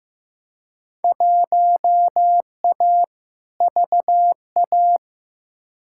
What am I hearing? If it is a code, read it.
Morse code, heard as a keyed tone, about 15 wpm, 700 Hz. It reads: 1A VA
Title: Field Day Exchanges - 15wpm